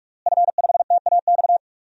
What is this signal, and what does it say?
Morse code, heard as a keyed tone, about 41 wpm, 700 Hz. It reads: F5TAX
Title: DXCC-100 Call Signs - 5 Characters - Repeated 2x - 45wpm